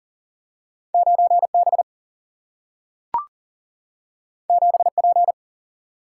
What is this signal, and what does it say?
Morse code, heard as a keyed tone, about 40 wpm, 700 Hz. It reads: 9B 7P
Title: Single-Letter Single-Number 40wpm